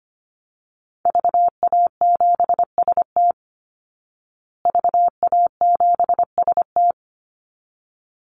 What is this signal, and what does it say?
Morse code, heard as a keyed tone, about 25 wpm, 700 Hz. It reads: VA7ST VA7ST
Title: Call Signs Repeated 3x - 25wpm